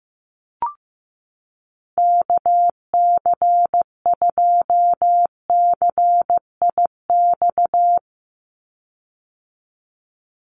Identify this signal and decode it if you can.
Morse code, heard as a keyed tone, about 15 wpm, 700 Hz. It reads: KC2CIX
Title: USA Call Signs 15wpm